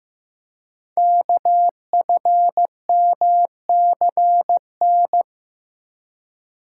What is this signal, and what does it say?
Morse code, heard as a keyed tone, about 15 wpm, 700 Hz. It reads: KFMCN
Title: Code Groups - 5 Letters - 15wpm